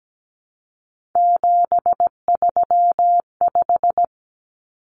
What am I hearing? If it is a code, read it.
Morse code, heard as a keyed tone, about 17 wpm, 700 Hz. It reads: 735
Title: Common Cut Numbers - 3-Digit 17wpm